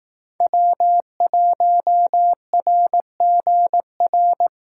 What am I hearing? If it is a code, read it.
Morse code, heard as a keyed tone, about 18 wpm, 700 Hz. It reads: W1RGR
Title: Speed Racing - USA 1x3 Call Signs 15wpm